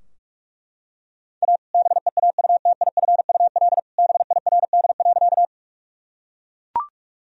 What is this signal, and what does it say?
Morse code, heard as a keyed tone, about 42 wpm, 700 Hz. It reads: A BEAUTIFUL BIRD.
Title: Sentences from Top 1000 Words - Review Part 2 - 45wpm